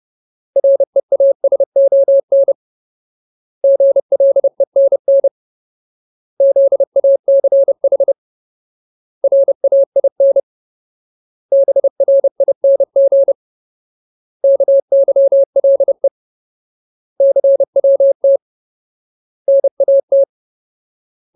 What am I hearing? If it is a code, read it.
Morse code, heard as a keyed tone, about 30 wpm, 550 Hz. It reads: REASON GLENN ZACH RAIN BRING KYLE CWT NAT
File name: Ultimate Word List / 30WPM Ultimate